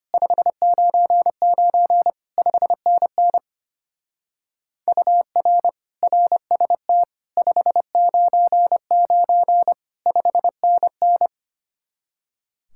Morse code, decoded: 599 5NN UR RST 599 5NN

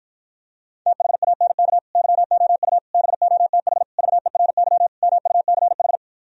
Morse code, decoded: THANK YOU BOTH VERY MUCH

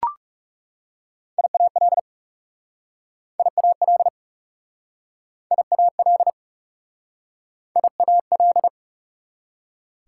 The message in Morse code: IAL IAL IAL IAL

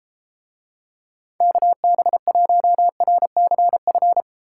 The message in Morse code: KB1RCF

